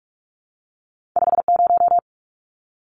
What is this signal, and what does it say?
Morse code, heard as a keyed tone, about 46 wpm, 700 Hz. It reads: 50
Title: Numbers - 2-Digit 45wpm